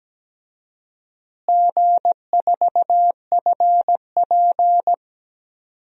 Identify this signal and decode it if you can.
Morse code, heard as a keyed tone, about 17 wpm, 700 Hz. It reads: G4FP
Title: DXCC-100 Call Signs - 4 Characters - Repeated 2x - 17wpm